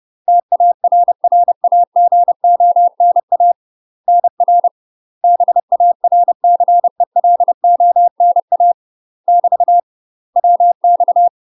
Morse code, heard as a keyed tone, about 30 wpm, 700 Hz. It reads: TARRAGONA NR BARCELONA <BT> WX
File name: UK MASTER QSOs TRAINING / 30WPM / UK MASTER QSOs 0036